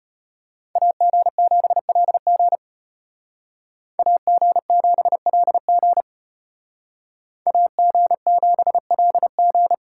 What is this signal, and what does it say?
Morse code, heard as a keyed tone, about 30 wpm, 700 Hz. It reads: AG7LG AG7LG AG7LG
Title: Speed Racing - USA 2x2 Call Signs 25wpm